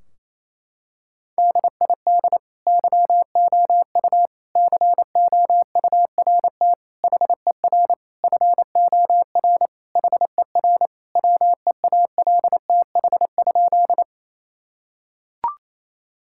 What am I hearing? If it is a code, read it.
Morse code, heard as a keyed tone, about 28 wpm, 700 Hz. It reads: DID YOU COURT HER FOR HER WEALTH?
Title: Sentences from Top 1600 Words 28wpm